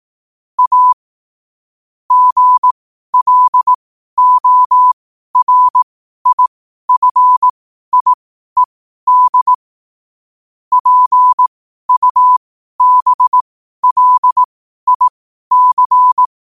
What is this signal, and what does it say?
Morse code, heard as a keyed tone, about 18 wpm, spaced as for 12 wpm, 1000 Hz. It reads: A GLORIFIED PUBLIC
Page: 12-wpm-morse-code-quotes